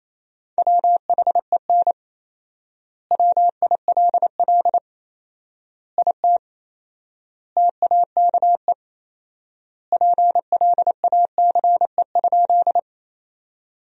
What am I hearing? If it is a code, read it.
Morse code, heard as a keyed tone, about 28 wpm, 700 Hz. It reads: WHEN WILL IT TAKE PLACE?
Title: 4x Word-Spacing - Sentences from Top 200 Words - 28wpm